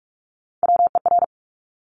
Morse code, heard as a keyed tone, about 46 wpm, 700 Hz. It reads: WER
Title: Top Three Letter Combinations in Words 45wpm